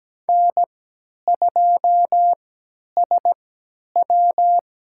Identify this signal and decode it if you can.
Morse code, heard as a keyed tone, about 17 wpm, spaced as for 10 wpm, 700 Hz. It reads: N2SW
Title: USA Call Signs - 3x Character Spacing - 17wpm